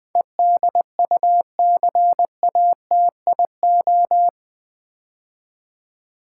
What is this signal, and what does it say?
Morse code, heard as a keyed tone, about 20 wpm, 700 Hz. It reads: EDUCATIO
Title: Build A Word - 4 Letters At A Time - 11-Letter Words - 20wpm